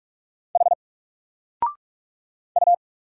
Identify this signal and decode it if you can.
Morse code, heard as a keyed tone, about 46 wpm, 700 Hz. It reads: H U
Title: Single Letter-Number 45wpm